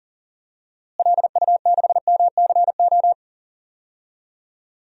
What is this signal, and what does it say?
Morse code, heard as a keyed tone, about 40 wpm, 700 Hz. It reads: LU6MCO
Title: DXCC-100 Call Signs - 6 Characters - Repeated 2x - 40wpm